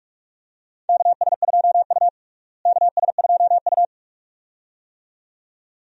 Morse code, heard as a keyed tone, about 46 wpm, 700 Hz. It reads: KS2U KS2U
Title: USA Call Signs - 4 Characters - Repeated 2x - 45wpm